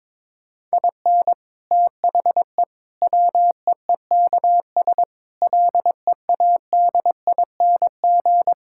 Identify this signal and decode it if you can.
Morse code, heard as a keyed tone, about 22 wpm, 700 Hz. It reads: IN THE WEEKS LEADING
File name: practice / 22wpm / Sets of 6 Words - 22wpm